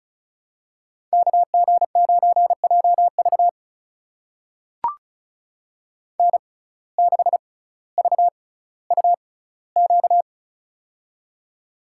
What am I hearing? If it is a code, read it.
Morse code, heard as a keyed tone, about 36 wpm, spaced as for 18 wpm, 700 Hz. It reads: KG9JV N6VUQ
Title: USA Call Signs - 5 Characters - 6x Character Spacing - 35wpm